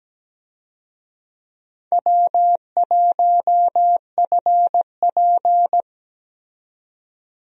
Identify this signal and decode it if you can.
Morse code, heard as a keyed tone, about 17 wpm, 700 Hz. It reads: W1FP
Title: USA Call Signs - 4 Characters - 17wpm